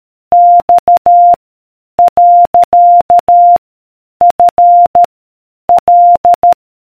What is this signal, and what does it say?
Morse code, heard as a keyed tone, about 13 wpm, 700 Hz. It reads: X . F L